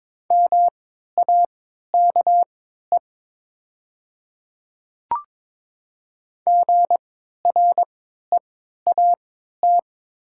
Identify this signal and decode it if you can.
Morse code, heard as a keyed tone, about 23 wpm, spaced as for 13 wpm, 700 Hz. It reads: MAKE GREAT
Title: 3x Character Spacing - Top 300 Words - 22wpm